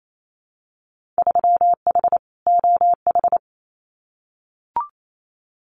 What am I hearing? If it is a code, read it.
Morse code, heard as a keyed tone, about 28 wpm, 700 Hz. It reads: 3H OH
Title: Winter Field Day Exchanges Encore 28wpm